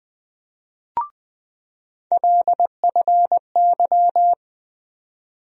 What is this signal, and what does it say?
Morse code, heard as a keyed tone, about 20 wpm, 700 Hz. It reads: LFY
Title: USA Call Sign Postfixes - 3 Letters 20wpm